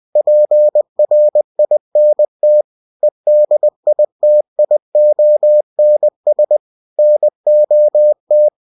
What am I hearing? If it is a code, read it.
Morse code, heard as a keyed tone, about 20 wpm, 600 Hz. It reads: PRINT EDITIONS NOT
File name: cw-books / winnie-the-pooh / winnie-20wpm-page-0112